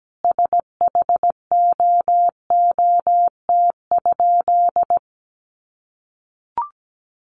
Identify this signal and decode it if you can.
Morse code, heard as a keyed tone, about 17 wpm, 700 Hz. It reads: SHOOT?
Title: A Sound Of Thunder 17wpm